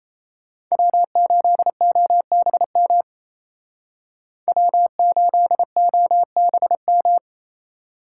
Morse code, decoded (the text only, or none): W8OBM W8OBM